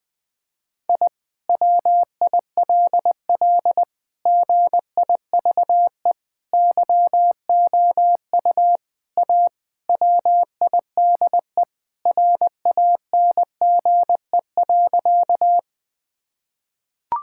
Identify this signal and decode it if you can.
Morse code, heard as a keyed tone, about 20 wpm, 700 Hz. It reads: I WILL GIVE YOU A WIDE RANGE.
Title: Sentences from Top 1000 Words - Review Part 6 - 20wpm